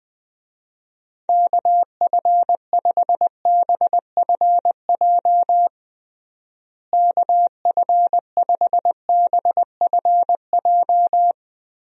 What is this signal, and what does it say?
Morse code, heard as a keyed tone, about 20 wpm, 700 Hz. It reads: KF5BFJ KF5BFJ
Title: USA Call Signs - 6 Characters - Repeated 3x - 20wpm